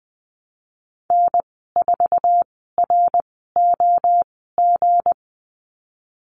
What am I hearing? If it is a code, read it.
Morse code, heard as a keyed tone, about 20 wpm, spaced as for 15 wpm, 700 Hz. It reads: N4ROG
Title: 167 Lesson 27g - Number 7 - Call Signs 20wpm